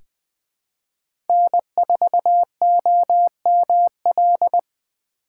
Morse code, heard as a keyed tone, about 20 wpm, 700 Hz. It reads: N4OML
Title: USA Call Signs - 5 Characters - 3x Character Spacing - 20wpm